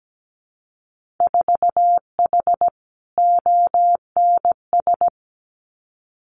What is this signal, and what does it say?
Morse code, heard as a keyed tone, about 17 wpm, 700 Hz. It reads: 4H ONS
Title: Winter Field Day Exchanges Encore 17wpm